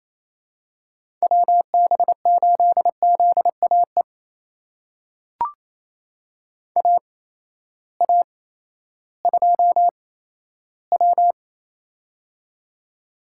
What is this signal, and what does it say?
Morse code, heard as a keyed tone, about 28 wpm, spaced as for 14 wpm, 700 Hz. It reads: WB8ZAE AA2W
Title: USA Call Signs - 8x Character Spacing - 28wpm